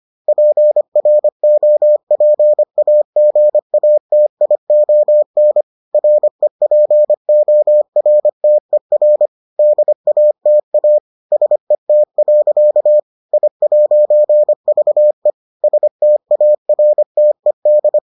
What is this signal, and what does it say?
Morse code, heard as a keyed tone, about 25 wpm, 600 Hz. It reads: PROPAGATION REPORTER DATA SET. I'VE STARTED